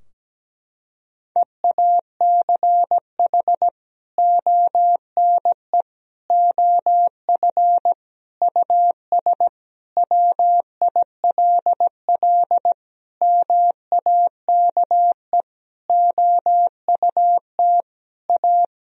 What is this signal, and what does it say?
Morse code, heard as a keyed tone, about 17 wpm, 700 Hz. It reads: EACH ONE OF US WILL MAKE OUT A
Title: Sentences from Top 1000 Words - Review Part 2 - 17wpm